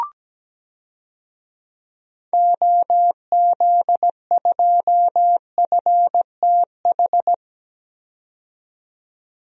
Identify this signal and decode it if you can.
Morse code, heard as a keyed tone, about 17 wpm, 700 Hz. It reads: OZ2FTH
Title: DXCC-100 Call Signs - 6 Characters - 17wpm